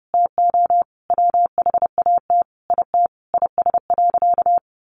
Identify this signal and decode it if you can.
Morse code, heard as a keyed tone, about 30 wpm, 700 Hz. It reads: TO WHAT IT IS.
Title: Sentences from Top 1500 Words 30wpm